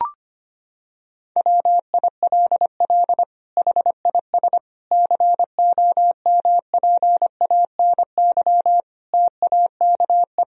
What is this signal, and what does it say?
Morse code, heard as a keyed tone, about 25 wpm, 700 Hz. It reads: WILL HIS COMPANY TAKE
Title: Sentences from Top 1000 Words - Review Part 4 - 25wpm